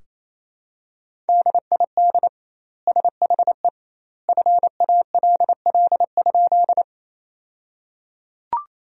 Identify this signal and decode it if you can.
Morse code, heard as a keyed tone, about 28 wpm, 700 Hz. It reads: DID SHE FALL?
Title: 2x Word-Spacing - Sentences from Top 500 Words - 28wpm